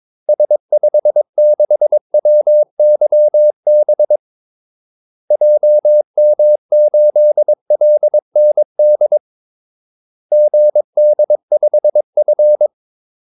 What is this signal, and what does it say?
Morse code, heard as a keyed tone, about 22 wpm, 600 Hz. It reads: S56WYB JM8LND GD5F
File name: Callsigns-22wpm